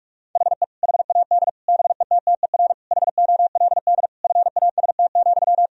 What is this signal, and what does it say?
Morse code, heard as a keyed tone, about 46 wpm, 700 Hz. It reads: HE HAD BETTER HOLD FAST,
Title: Sentences from Top 1000 Words - Review Part 2 - 45wpm